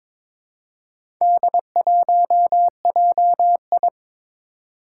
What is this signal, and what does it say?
Morse code, heard as a keyed tone, about 22 wpm, 700 Hz. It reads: D1JI
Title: Four Challenging Letter-Number Combinations 22wpm